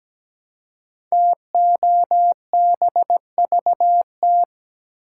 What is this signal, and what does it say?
Morse code, heard as a keyed tone, about 17 wpm, 700 Hz. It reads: TOBVT
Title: Code Groups - 5 Letters - 17wpm